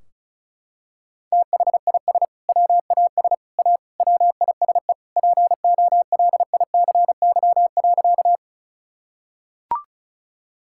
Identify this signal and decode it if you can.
Morse code, heard as a keyed tone, about 34 wpm, 700 Hz. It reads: THIS WAS A WISE POLICY.
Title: Sentences from Top 1600 Words 35wpm